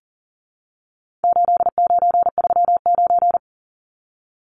Morse code, heard as a keyed tone, about 40 wpm, 700 Hz. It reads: 8939
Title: Numbers - 4-Digit 40wpm